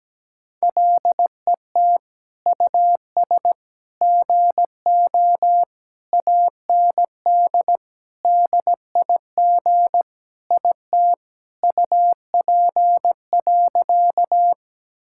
Morse code, heard as a keyed tone, about 17 wpm, 700 Hz. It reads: LET US GO AND DIG IT UP.
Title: Sentences from Top 1600 Words 17wpm